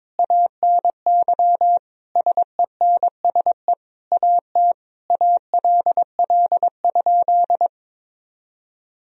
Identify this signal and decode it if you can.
Morse code, heard as a keyed tone, about 22 wpm, 700 Hz. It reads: ANY SENSE AT ALL?
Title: Sentences from Top 1000 Words - Review Part 4 - 22wpm